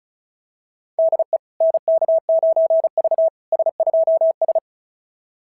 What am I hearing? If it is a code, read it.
Morse code, heard as a keyed tone, about 36 wpm, 690 Hz. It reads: DE NK9V S2S
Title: SOTA QSO Practice 35wpm